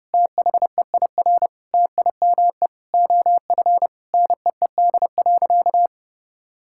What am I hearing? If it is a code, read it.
Morse code, heard as a keyed tone, about 30 wpm, 700 Hz. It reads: THEIR TIME OF NEED.
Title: Sentences from Top 1000 Words - Review Part 5 - 30wpm